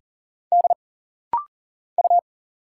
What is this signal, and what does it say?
Morse code, heard as a keyed tone, about 40 wpm, 700 Hz. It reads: D U
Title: Single Letter-Number Alt - Rapid-Fire 40wpm